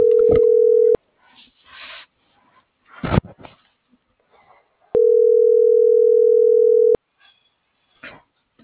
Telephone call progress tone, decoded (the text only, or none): ringback tone